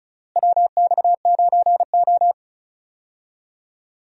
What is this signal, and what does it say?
Morse code, heard as a keyed tone, about 36 wpm, 700 Hz. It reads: WX9O
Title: USA Call Signs - 4 Characters - Repeated 2x - 35wpm